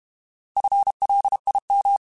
Morse code, heard as a keyed tone, about 32 wpm, 800 Hz. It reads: FLIM
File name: Alphabetical / Alphabetical mp3s / Fs @30